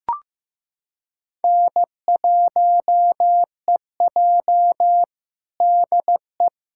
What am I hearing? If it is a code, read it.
Morse code, heard as a keyed tone, about 15 wpm, 700 Hz. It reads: N1EJ DE
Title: USA Callsign DE Callsign - 4 Characters - 15wpm